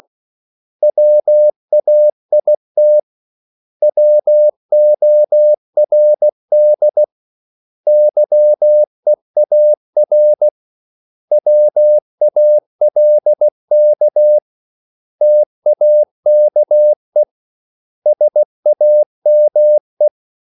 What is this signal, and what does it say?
Morse code, heard as a keyed tone, about 16 wpm, 600 Hz. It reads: WAIT WORD YEAR WALK TAKE SAME
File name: CW / Practice / 500MostCommonEnglishWords16